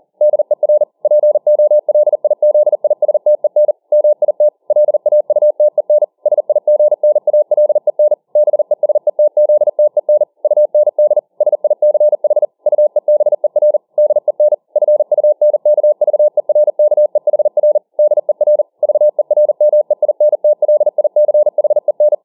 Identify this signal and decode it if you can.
Morse code, heard as a keyed tone, about 40 wpm, 600 Hz. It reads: DER POLIZISTEN MIT LAUTEN SIGNALEN BESETZTEN UND SICH UEBER DEN FUNKVERKEHR DER VERMEINTLICHEN